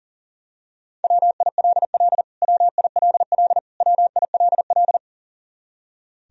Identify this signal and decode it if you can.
Morse code, heard as a keyed tone, about 40 wpm, 700 Hz. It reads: WILL WILL WILL
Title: Top 100 Words Repeated 40wpm